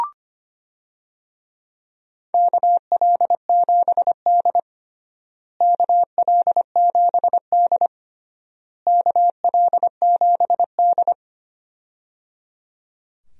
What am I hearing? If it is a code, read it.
Morse code, heard as a keyed tone, about 25 wpm, 700 Hz. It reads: KL7D KL7D KL7D
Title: DXCC-100 Call Signs - 4 Characters - Repeated 3x - 25wpm